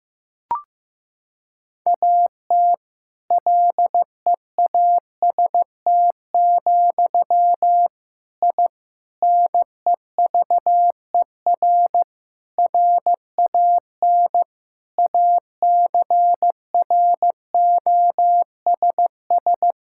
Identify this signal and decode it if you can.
Morse code, heard as a keyed tone, about 15 wpm, 700 Hz. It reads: AT LEAST, I NEVER RAN ACROSS
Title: Sentences from Top 1000 Words - Review Part 5 - 15wpm